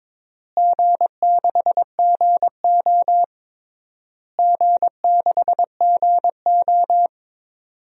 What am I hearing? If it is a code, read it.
Morse code, heard as a keyed tone, about 22 wpm, 700 Hz. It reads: G6GO G6GO
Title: DXCC-100 Call Signs - 4 Characters - Repeated 3x - 22wpm